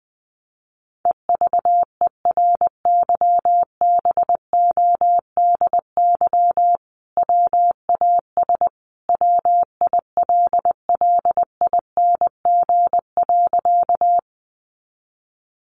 Morse code, heard as a keyed tone, about 20 wpm, 700 Hz. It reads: EVERYBODY WAS WILLING.
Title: Sentences from Top 1400 Words 20wpm